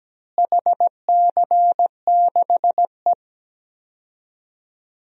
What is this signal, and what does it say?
Morse code, heard as a keyed tone, about 17 wpm, 700 Hz. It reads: HC6E
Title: DXCC-100 Call Signs - 4 Characters - Repeated 3x - 17wpm